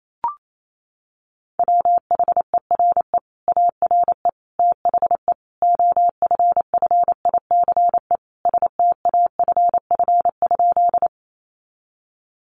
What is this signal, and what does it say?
Morse code, heard as a keyed tone, about 28 wpm, 700 Hz. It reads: WHERE ARE THE OFFICE STAFF?
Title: Sentences from Top 1200 Words 28wpm